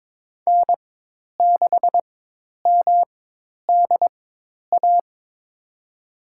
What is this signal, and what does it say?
Morse code, heard as a keyed tone, about 22 wpm, spaced as for 10 wpm, 700 Hz. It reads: N6MDA
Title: USA Call Signs - 5 Characters - 4x Character Spacing - 22wpm